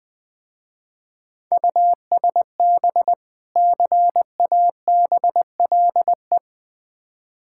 Morse code, heard as a keyed tone, about 20 wpm, 700 Hz. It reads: USB CABLE
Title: Difficult Words with Hints - 20wpm